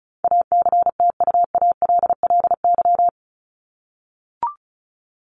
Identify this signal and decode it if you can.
Morse code, heard as a keyed tone, about 34 wpm, 700 Hz. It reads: ACTUALLY
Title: Top 501-750 Words 35wpm